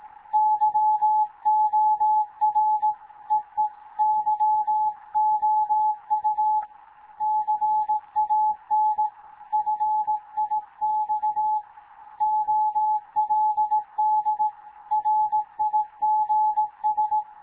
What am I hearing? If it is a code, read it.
Morse code, heard as a keyed tone, about 18 wpm, 860 Hz. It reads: YOR EE YOU CAN FIX OLD RIGS
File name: Morze-HA2MN / hangp / 2 11 GYAKORLAT